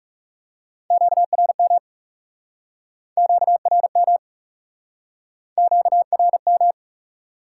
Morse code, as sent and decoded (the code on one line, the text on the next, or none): --.- .-. -- / --.- .-. -- / --.- .-. --
QRM QRM QRM